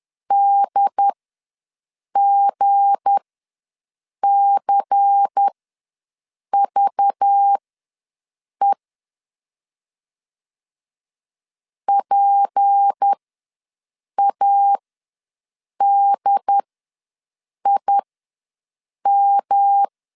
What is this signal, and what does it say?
Morse code, heard as a keyed tone, about 10 wpm, spaced as for 6 wpm, 800 Hz. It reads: DGCVE PADIM
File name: LEZIONI / 8 YJ45 / 8 YJ ws 10wpm 4043